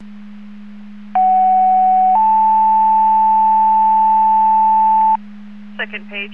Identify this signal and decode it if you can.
Two-tone page, heard as a one-second tone, then a three-second tone, about 750 then 880 Hz, with speech. It is heard from 1.2 to 5.2 s.